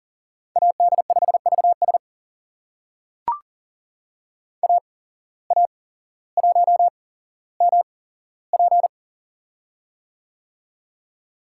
Morse code, heard as a keyed tone, about 41 wpm, spaced as for 17 wpm, 700 Hz. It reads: AD5VS AA1MP
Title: USA Call Signs - 8x Character Spacing - 40wpm